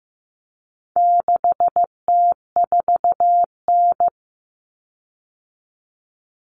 Morse code, heard as a keyed tone, about 15 wpm, 700 Hz. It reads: 6T4N
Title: Common Cut Numbers - 4-Digit 15wpm